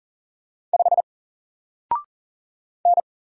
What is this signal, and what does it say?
Morse code, heard as a keyed tone, about 40 wpm, 700 Hz. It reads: 5 N
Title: Single Letter-Number Alt 40wpm